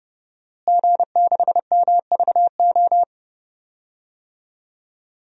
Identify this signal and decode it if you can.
Morse code, heard as a keyed tone, about 30 wpm, 700 Hz. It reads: G6MVO
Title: DXCC-100 Call Signs - 5 Characters - Repeated 2x - 30wpm